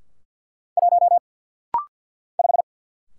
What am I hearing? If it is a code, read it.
Morse code, heard as a keyed tone, about 47 wpm, 700 Hz. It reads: 1 5
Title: Single Letter-Number - Rapid-Fire 50wpm